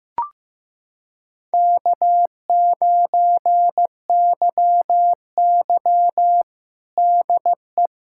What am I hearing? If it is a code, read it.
Morse code, heard as a keyed tone, about 15 wpm, 700 Hz. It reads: K9YY DE
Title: USA Callsign DE Callsign - 4 Characters - 15wpm